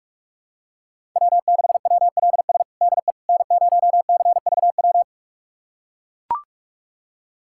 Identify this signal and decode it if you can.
Morse code, heard as a keyed tone, about 43 wpm, 700 Hz. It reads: W6WLS DE N0CVW
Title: USA Callsign DE Callsign - 5 Characters - 45wpm